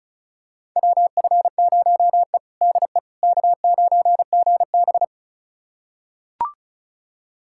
Morse code, heard as a keyed tone, about 36 wpm, 700 Hz. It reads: WF0E DE K9GB
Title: USA Callsign DE Callsign - 4 Characters - 35wpm